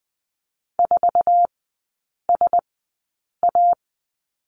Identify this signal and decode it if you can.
Morse code, heard as a keyed tone, about 20 wpm, spaced as for 8 wpm, 700 Hz. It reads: 4SA
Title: 074 Lesson 13e - Letter L - Three Characters 20wpm